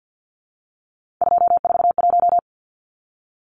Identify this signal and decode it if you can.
Morse code, heard as a keyed tone, about 49 wpm, 700 Hz. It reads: 241
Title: Numbers - 3-Digit 50wpm